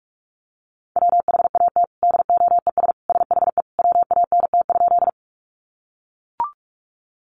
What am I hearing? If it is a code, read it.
Morse code, heard as a keyed tone, about 44 wpm, 700 Hz. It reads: WHAT DOES SHE WANT?
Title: Sentences from Top 200 Words 45wpm